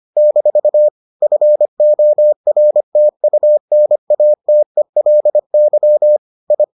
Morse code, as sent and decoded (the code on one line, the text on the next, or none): -....- / ..-. --- .-. - ..- -. .- - . .-.. -.-- / ..
- FORTUNATELY I